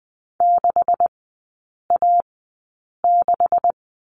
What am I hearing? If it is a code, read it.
Morse code, heard as a keyed tone, about 20 wpm, spaced as for 8 wpm, 700 Hz. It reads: 6A6
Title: 152 Lesson 25e - Letter p - Three Characters 20wpm